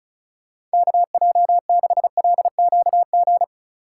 Morse code, heard as a keyed tone, about 36 wpm, 700 Hz. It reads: KJ6LQG
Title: USA Call Signs - 6 Characters - 2x Character Spacing - 35wpm